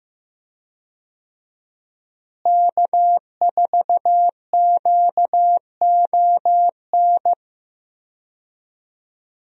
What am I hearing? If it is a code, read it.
Morse code, heard as a keyed tone, about 15 wpm, 700 Hz. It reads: K4QON
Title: USA Call Signs - 5 Characters - Repeated 2x - 15wpm